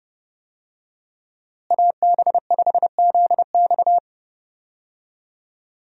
Morse code, heard as a keyed tone, about 30 wpm, 700 Hz. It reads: AB5ZX
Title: USA Call Signs - 5 Characters - Repeated 3x - 30wpm